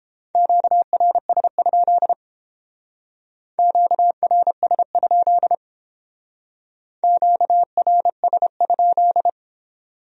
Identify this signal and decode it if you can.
Morse code, heard as a keyed tone, about 29 wpm, 700 Hz. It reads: QRS? QRS? QRS?